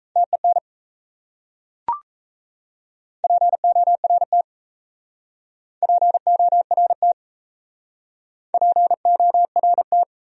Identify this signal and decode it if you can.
Morse code, heard as a keyed tone, about 37 wpm, 700 Hz. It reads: TEN PORT PORT PORT